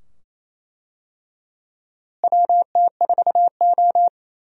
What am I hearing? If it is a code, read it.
Morse code, heard as a keyed tone, about 28 wpm, 700 Hz. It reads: WT4O
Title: USA Call Signs - 4 Characters - Repeated 2x - 28wpm